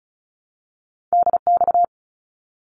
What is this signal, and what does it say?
Morse code, heard as a keyed tone, about 34 wpm, 700 Hz. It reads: DX
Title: ARRL Sections 35wpm